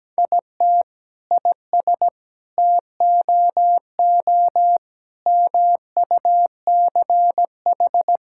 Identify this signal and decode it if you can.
Morse code, heard as a keyed tone, about 17 wpm, 700 Hz. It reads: IT IS TOO MUCH
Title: Sentences from Top 1000 Words - Review Part 5 - 17wpm